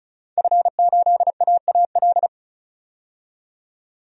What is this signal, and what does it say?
Morse code, heard as a keyed tone, about 36 wpm, 700 Hz. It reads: F8AAL
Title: DXCC-100 Call Signs - 5 Characters - 35wpm